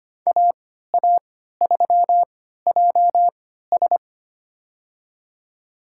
Morse code, heard as a keyed tone, about 25 wpm, spaced as for 14 wpm, 700 Hz. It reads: AA3JS
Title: USA Call Signs - 5 Characters - 3x Character Spacing - 25wpm